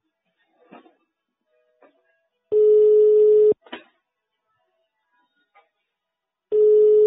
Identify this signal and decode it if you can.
Ringback tone (425 Hz, 1 s on, 3 s off).
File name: sauvegarde / ACRRecordings- -jusquau 17-juin-2017 / Police De Huy 2017 05 24 09 27 04